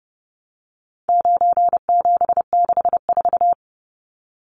Morse code, heard as a keyed tone, about 30 wpm, 700 Hz. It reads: 9764
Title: Numbers - 4-Digit 30wpm